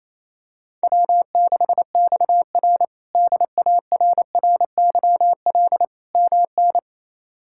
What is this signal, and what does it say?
Morse code, heard as a keyed tone, about 28 wpm, 700 Hz. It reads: W6XR DARRYL MN
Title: K1USN SST QSO Practice - 28wpm